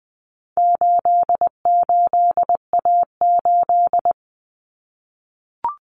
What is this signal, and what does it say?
Morse code, heard as a keyed tone, about 20 wpm, 700 Hz. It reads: 88A8
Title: Common Cut Numbers - 4-Digit 20wpm